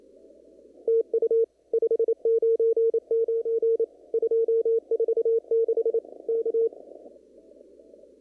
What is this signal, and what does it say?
Morse code, heard as a keyed tone, about 28 wpm, 460 Hz. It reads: TU 599 246 K
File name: CW / kt5j wpx cw-op-k5pi